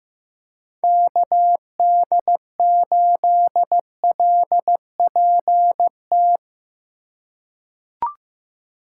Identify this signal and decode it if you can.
Morse code, heard as a keyed tone, about 15 wpm, 700 Hz. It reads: KD8LPT
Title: USA Call Signs - 4x Character Spacing - 15wpm